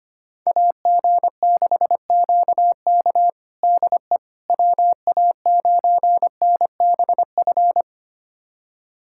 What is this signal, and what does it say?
Morse code, heard as a keyed tone, about 25 wpm, 700 Hz. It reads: AG6QK DE WA9NBF
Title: USA Callsign DE Callsign 25wpm